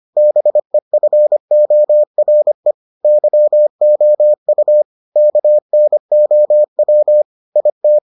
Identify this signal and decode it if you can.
Morse code, heard as a keyed tone, about 25 wpm, 600 Hz. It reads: BEFORE YOU KNOW IT